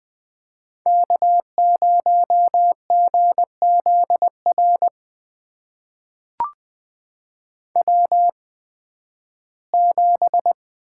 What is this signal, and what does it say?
Morse code, heard as a keyed tone, about 20 wpm, spaced as for 13 wpm, 700 Hz. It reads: K0GZR W7
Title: USA Call Signs - 5 Characters - 8x Character Spacing - 20wpm